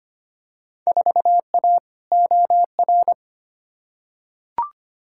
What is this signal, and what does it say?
Morse code, heard as a keyed tone, about 25 wpm, 700 Hz. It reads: 4A OR
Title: Field Day Exchanges Encore - 25wpm